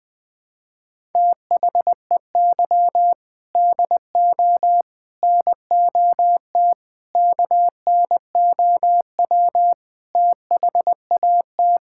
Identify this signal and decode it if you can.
Morse code, heard as a keyed tone, about 20 wpm, 700 Hz. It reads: THEY DO NOT KNOW THAT
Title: Sentences with 4-Word Phrases - 20wpm